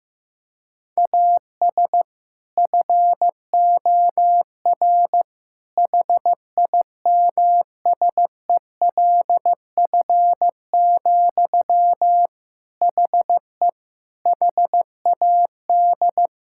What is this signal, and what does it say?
Morse code, heard as a keyed tone, about 15 wpm, 700 Hz. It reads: AS FOR HIMSELF, HE HAD
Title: Sentences from Top 1600 Words 15wpm